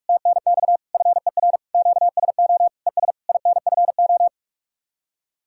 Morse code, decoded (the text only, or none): TNX FER QSO ES INFO